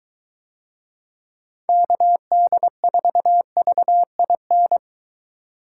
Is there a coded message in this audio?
Yes: Morse code from 1.7 to 4.8 s, heard as a keyed tone, about 22 wpm, 700 Hz. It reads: KD4VIN